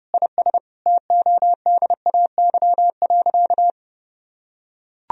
Morse code, heard as a keyed tone, about 30 wpm, 700 Hz. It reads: IS TODAY.
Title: Sentences from Top 1000 Words - Review Part 3 - 30wpm